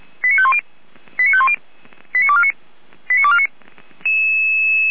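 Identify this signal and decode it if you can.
Five-tone selective call (ZVEI), heard as quick runs of five tones, tones of about 70 ms. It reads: 89510 89510 89249 89249
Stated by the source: ZVEI five-tone selective call, code 89310, received via SDR.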